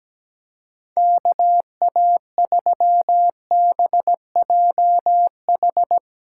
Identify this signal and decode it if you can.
Morse code, heard as a keyed tone, about 17 wpm, 700 Hz. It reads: KA3BJH
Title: USA Callsign DE Callsign - 6 Characters - 17wpm